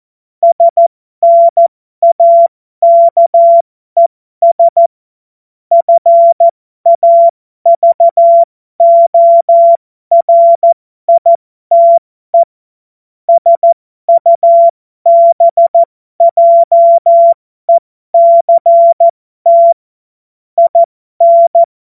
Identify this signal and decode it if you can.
Morse code, heard as a keyed tone, about 14 wpm, spaced as for 12 wpm, 680 Hz. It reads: SNAKES FAVORITE SUBJECT IN
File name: Morsekurs / Witze